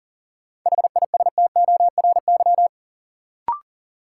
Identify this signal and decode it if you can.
Morse code, heard as a keyed tone, about 40 wpm, 700 Hz. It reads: HISTORY
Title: Sets of 3 Words - New Word Drill - 40wpm